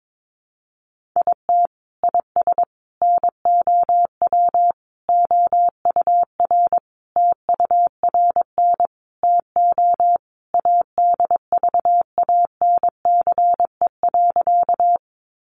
Morse code, heard as a keyed tone, about 22 wpm, 700 Hz. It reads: IT IS NOW OUR TURN TO ADVANCE.
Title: Sentences from Top 1400 Words 22wpm